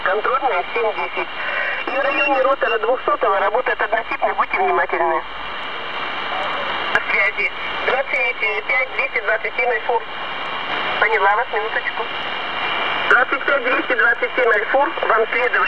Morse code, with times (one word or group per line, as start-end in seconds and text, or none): none